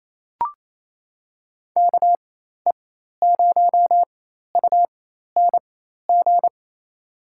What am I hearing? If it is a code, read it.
Morse code, heard as a keyed tone, about 28 wpm, spaced as for 13 wpm, 700 Hz. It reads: KE0UNG
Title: USA Call Signs - 4x Character Spacing - 28wpm